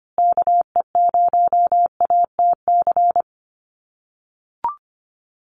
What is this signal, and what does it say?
Morse code, heard as a keyed tone, about 25 wpm, 700 Hz. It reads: KE0ATC
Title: USA Call Signs Repeated 3x 25wpm